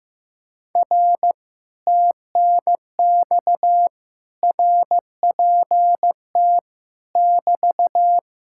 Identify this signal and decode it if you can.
Morse code, heard as a keyed tone, about 15 wpm, 700 Hz. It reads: R TNX RPT <BT>
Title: Speed-Racing Common QSO Phrases Encore - 15wpm